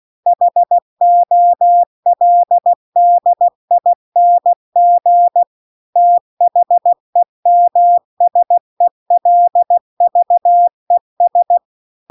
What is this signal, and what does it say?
Morse code, heard as a keyed tone, about 16 wpm, 700 Hz. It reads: HOLDING THEMSELVES
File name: CW-The-Big-Twoheaded-River-1-16WPM